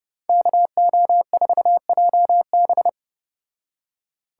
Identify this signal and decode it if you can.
Morse code, heard as a keyed tone, about 34 wpm, 700 Hz. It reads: KO4JB